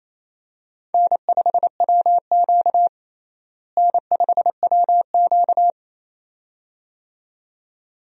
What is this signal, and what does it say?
Morse code, heard as a keyed tone, about 28 wpm, 700 Hz. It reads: N5WQ N5WQ
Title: USA Call Signs - 4 Characters - Repeated 2x - 28wpm